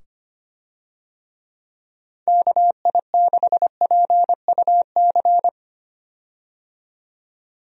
Morse code, heard as a keyed tone, about 25 wpm, 700 Hz. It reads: KI6PUC
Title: USA Call Signs - 6 Characters - 25wpm